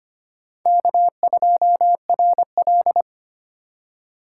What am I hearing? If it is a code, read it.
Morse code, heard as a keyed tone, about 32 wpm, 700 Hz. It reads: K2RL